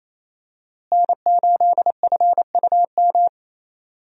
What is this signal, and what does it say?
Morse code, heard as a keyed tone, about 28 wpm, 700 Hz. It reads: N8FUM
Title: USA Call Signs - 5 Characters - Repeated 2x - 28wpm